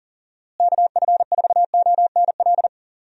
Morse code, heard as a keyed tone, about 40 wpm, 700 Hz. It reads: KF4ONL
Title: USA Call Signs - 6 Characters - Repeated 2x - 40wpm